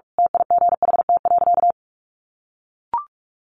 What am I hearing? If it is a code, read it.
Morse code, heard as a keyed tone, about 44 wpm, 700 Hz. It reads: TIGHT.
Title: A Sound Of Thunder 45wpm